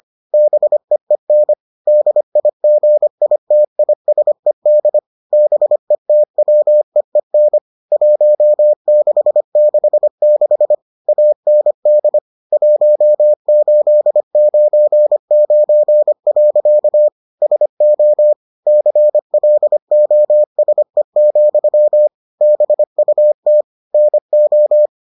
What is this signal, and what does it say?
Morse code, heard as a keyed tone, about 25 wpm, 600 Hz. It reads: BEEN DIGITISED BETWEEN 1666 AND 1899. SO CLOSE, BUT NO